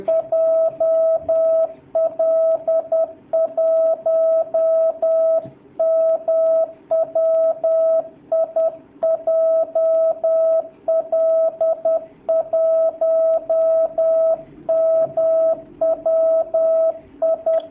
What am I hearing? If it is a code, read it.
Morse code, heard as a keyed tone, about 10 wpm, 650 Hz. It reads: JL1MWI JL1MWI
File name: jh8cbh / 2019cw / sp10- w